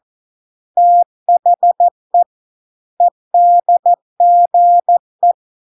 Morse code, heard as a keyed tone, about 14 wpm, 700 Hz. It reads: THE EDGE